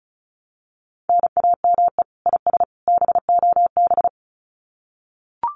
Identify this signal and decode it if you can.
Morse code, heard as a keyed tone, about 34 wpm, 700 Hz. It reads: NAME IS BOB
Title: Common QSO Phrases 35wpm